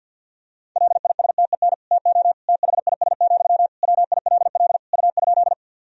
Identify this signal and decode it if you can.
Morse code, heard as a keyed tone, about 50 wpm, 700 Hz. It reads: LISTEN TO THIS, WILL U?
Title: Sentences from Top 400 Words and Abbreviations 50wpm